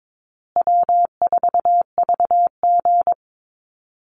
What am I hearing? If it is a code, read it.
Morse code, heard as a keyed tone, about 22 wpm, 700 Hz. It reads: W4VG
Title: Call Signs Repeated 3x - 22wpm